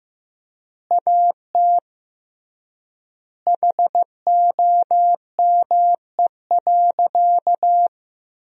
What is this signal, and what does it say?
Morse code, heard as a keyed tone, about 15 wpm, 700 Hz. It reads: AT HOME.
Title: 3x Word-Spacing - Sentences from Top 400 Words - 15wpm